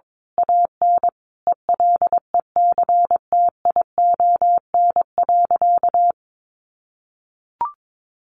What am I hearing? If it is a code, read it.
Morse code, heard as a keyed tone, about 22 wpm, 700 Hz. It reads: AN ELECTION.
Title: Sentences from Top 1500 Words 22wpm